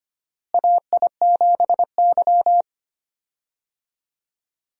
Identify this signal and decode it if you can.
Morse code, heard as a keyed tone, about 25 wpm, 700 Hz. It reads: AI7Y
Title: USA Call Signs - 4 Characters - Repeated 3x - 25wpm